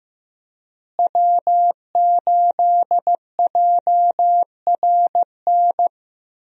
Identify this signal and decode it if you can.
Morse code, heard as a keyed tone, about 16 wpm, 700 Hz. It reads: W8JRN
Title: USA Call Signs - 5 Characters - Repeated 3x - 15wpm